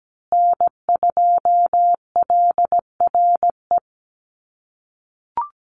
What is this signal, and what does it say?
Morse code, heard as a keyed tone, about 17 wpm, 700 Hz. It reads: N2LRE
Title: USA Call Signs Repeated 3x 17wpm